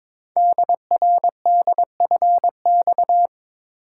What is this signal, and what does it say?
Morse code, heard as a keyed tone, about 22 wpm, 700 Hz. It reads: DRDFX
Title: Code Groups - 5 Letters - 22wpm